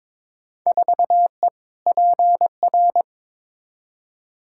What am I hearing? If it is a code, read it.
Morse code, heard as a keyed tone, about 22 wpm, 700 Hz. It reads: 4E PR
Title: Field Day Exchanges Encore - 22wpm